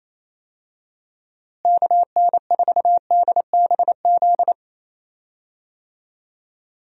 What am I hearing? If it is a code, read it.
Morse code, heard as a keyed tone, about 28 wpm, 700 Hz. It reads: KN4DBZ
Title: USA Call Signs - 6 Characters - Repeated 2x - 28wpm